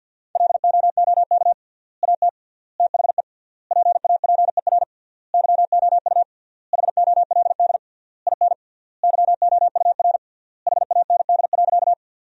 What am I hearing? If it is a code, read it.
Morse code, heard as a keyed tone, about 52 wpm, 700 Hz. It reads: LOOK AT THE PAPER YOU HOLD IN YOUR HAND.
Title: 3x Word-Spacing - Sentences from Top 400 Words - 50wpm